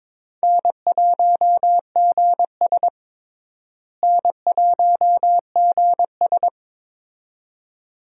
Morse code, heard as a keyed tone, about 22 wpm, 700 Hz. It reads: N1GS N1GS
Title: USA Call Signs - 4 Characters - Repeated 2x - 22wpm